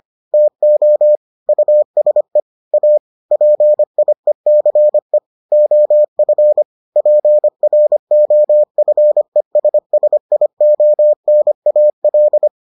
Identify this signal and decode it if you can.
Morse code, heard as a keyed tone, about 25 wpm, 600 Hz. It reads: TO USE A PIECE OF PROFESSIONAL